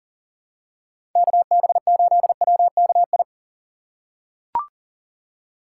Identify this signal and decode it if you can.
Morse code, heard as a keyed tone, about 40 wpm, 700 Hz. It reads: KB8WKI
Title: USA Call Signs - 6 Characters - 2x Character Spacing - 40wpm